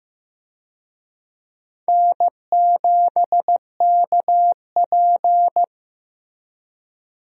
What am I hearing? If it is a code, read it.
Morse code, heard as a keyed tone, about 15 wpm, 700 Hz. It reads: N7KP
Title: USA Call Signs - 4 Characters - Repeated 2x - 15wpm